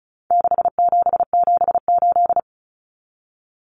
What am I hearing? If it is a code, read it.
Morse code, heard as a keyed tone, about 34 wpm, 700 Hz. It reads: 6778
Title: Numbers - 4-Digit 35wpm